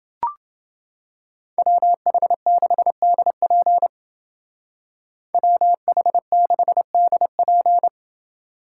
Morse code, heard as a keyed tone, about 29 wpm, 700 Hz. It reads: WH6DP WH6DP